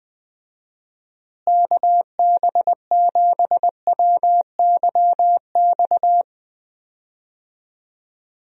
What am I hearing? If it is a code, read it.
Morse code, heard as a keyed tone, about 21 wpm, 700 Hz. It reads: KB7WYX
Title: USA Call Signs - 6 Characters - Repeated 2x - 20wpm